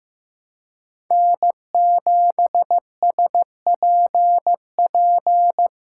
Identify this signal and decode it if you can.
Morse code, heard as a keyed tone, about 15 wpm, 700 Hz. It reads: N7SPP
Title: USA Call Signs - 5 Characters - 4x Character Spacing - 15wpm